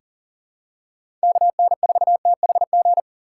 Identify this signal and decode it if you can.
Morse code, heard as a keyed tone, about 40 wpm, 700 Hz. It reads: KN4THG
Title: USA Call Signs - 6 Characters - 2x Character Spacing - 40wpm